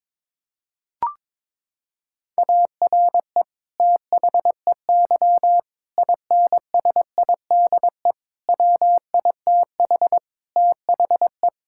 Morse code, heard as a keyed tone, about 22 wpm, 700 Hz. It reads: ARE THEY INSIDE WITH THE
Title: Sentences from Top 1000 Words - Review Part 6 - 22wpm